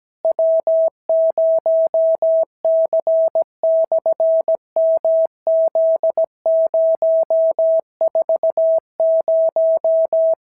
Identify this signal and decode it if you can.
Morse code, heard as a keyed tone, about 16 wpm, 650 Hz. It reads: W0C/MZ040
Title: SOTA QSO Practice 17wpm